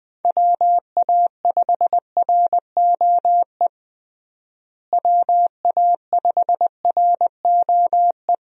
Morse code, heard as a keyed tone, about 21 wpm, 700 Hz. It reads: WA5ROE WA5ROE
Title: USA Call Signs - 6 Characters - Repeated 3x - 20wpm